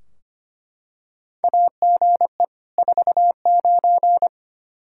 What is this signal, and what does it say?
Morse code, heard as a keyed tone, about 25 wpm, 700 Hz. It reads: AGE 49
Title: K1USN SST QSO Practice 25wpm